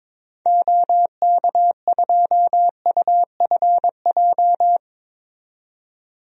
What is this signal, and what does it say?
Morse code, heard as a keyed tone, about 22 wpm, 700 Hz. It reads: OK2UFJ
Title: DXCC-100 Call Signs - 6 Characters - 22wpm